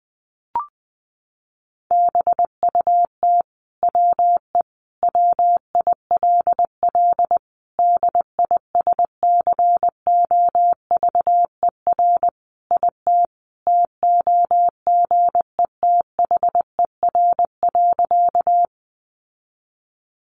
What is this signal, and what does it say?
Morse code, heard as a keyed tone, about 20 wpm, 700 Hz. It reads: BUT WE WILL DISCOVER IT TOGETHER.
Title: Sentences from Top 1100 Words 20wpm